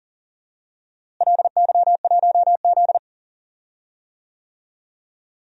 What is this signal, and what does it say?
Morse code, heard as a keyed tone, about 38 wpm, 700 Hz. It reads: LY1Z
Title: DXCC-100 Call Signs - 4 Characters - Repeated 2x - 40wpm